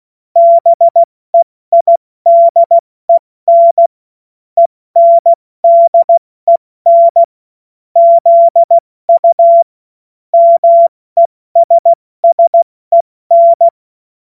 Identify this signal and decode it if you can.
Morse code, heard as a keyed tone, about 16 wpm, 680 Hz. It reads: BEIDEN ENDEN ZU MESSEN